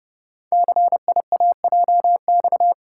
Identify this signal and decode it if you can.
Morse code, heard as a keyed tone, about 30 wpm, 690 Hz. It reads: CIAJX
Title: Code Groups - 5 Letters - 30wpm